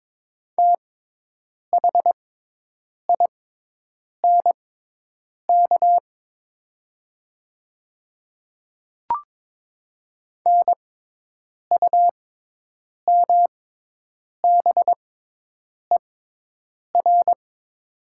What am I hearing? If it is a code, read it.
Morse code, heard as a keyed tone, about 22 wpm, spaced as for 8 wpm, 700 Hz. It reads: THINK NUMBER
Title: 6x Character Spacing - Top 300 Words - 22wpm